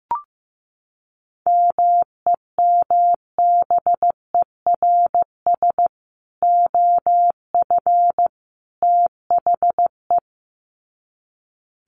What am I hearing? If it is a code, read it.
Morse code, heard as a keyed tone, about 15 wpm, 700 Hz. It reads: MEMBERS OF THE
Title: Sets of 3 Words 15wpm